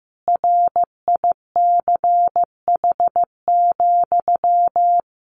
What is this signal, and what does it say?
Morse code, heard as a keyed tone, about 15 wpm, 700 Hz. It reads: RICH,
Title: Sentences from Top 1500 Words 15wpm